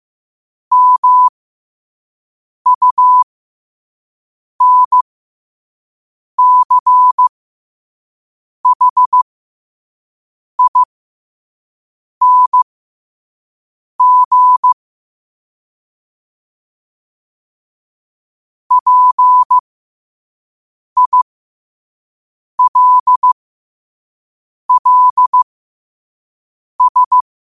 Morse code, heard as a keyed tone, about 15 wpm, spaced as for 5 wpm, 1000 Hz. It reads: MUNCHING PILLS
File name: qotdfiles / 5wpm